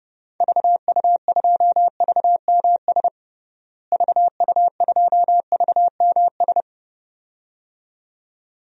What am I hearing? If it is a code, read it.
Morse code, heard as a keyed tone, about 30 wpm, 700 Hz. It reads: VU2VMS VU2VMS
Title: DXCC-100 Call Signs - 6 Characters - Repeated 2x - 30wpm